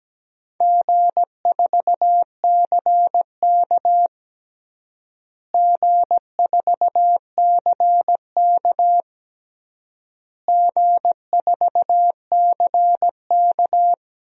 Morse code, heard as a keyed tone, about 17 wpm, 700 Hz. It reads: G4CK G4CK G4CK
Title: DXCC-100 Call Signs - 4 Characters - Repeated 3x - 17wpm